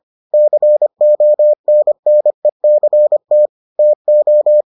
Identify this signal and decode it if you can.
Morse code, heard as a keyed tone, about 25 wpm, 600 Hz. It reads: CONNECT TO